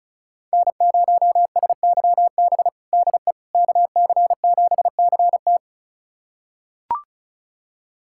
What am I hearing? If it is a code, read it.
Morse code, heard as a keyed tone, about 34 wpm, 700 Hz. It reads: N0SYB DE KC7CT
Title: USA Callsign DE Callsign - 5 Characters - 35wpm